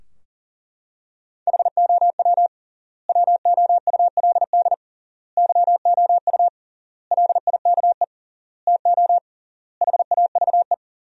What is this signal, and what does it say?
Morse code, heard as a keyed tone, about 40 wpm, 700 Hz. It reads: HOW WOULD YOU LIKE TO HAVE
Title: 3x Word-Spacing - Sentences from Top 100 Words - 40wpm